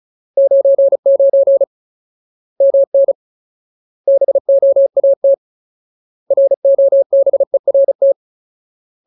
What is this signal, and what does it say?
Morse code, heard as a keyed tone, about 35 wpm, 550 Hz. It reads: 99 MN BOAT ROBERT
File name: Ultimate Word List / 35WPM Ultimate